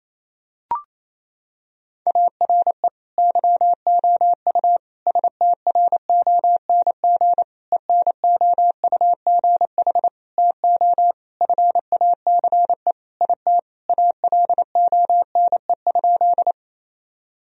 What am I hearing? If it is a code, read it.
Morse code, heard as a keyed tone, about 28 wpm, 700 Hz. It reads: ARE YOU STRONG ENOUGH TO FACE IT ALONE?
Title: Sentences from Top 1000 Words - Review Part 3 - 28wpm